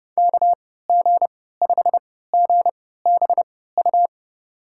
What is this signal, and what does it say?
Morse code, heard as a keyed tone, about 30 wpm, spaced as for 17 wpm, 700 Hz. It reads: KG5GBU
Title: USA Call Signs - 3x Character Spacing - 30wpm